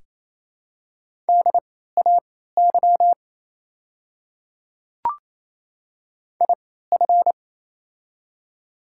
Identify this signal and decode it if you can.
Morse code, heard as a keyed tone, about 28 wpm, spaced as for 16 wpm, 700 Hz. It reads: DAY IF
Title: 3x Character Spacing - Top 300 Words - 28wpm